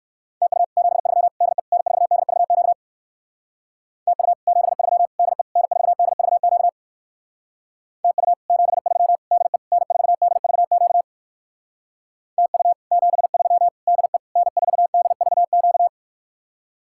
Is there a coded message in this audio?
Yes: Morse code from 0.4 to 15.9 s, heard as a keyed tone, about 59 wpm, 700 Hz. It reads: TU 73 DE N4DVQ TU 73 DE N4DVQ TU 73 DE N4DVQ TU 73 DE N4DVQ